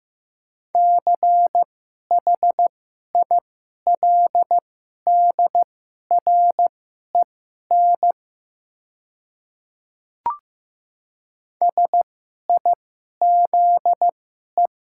Morse code, heard as a keyed tone, about 15 wpm, spaced as for 11 wpm, 690 Hz. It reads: CHILDREN SIZE
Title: 2x Character Spacing - Top 500 Words - 15wpm